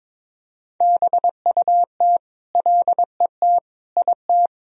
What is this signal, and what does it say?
Morse code, heard as a keyed tone, about 22 wpm, 700 Hz. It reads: BUT LET IT
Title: Sentences from Top 1000 Words - Review Part 1 - 22wpm